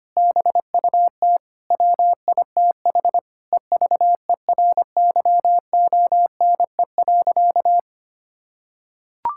Morse code, heard as a keyed tone, about 25 wpm, 700 Hz. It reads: BUT WITH EVERYONE.
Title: Sentences from Top 1000 Words - Review Part 5 - 25wpm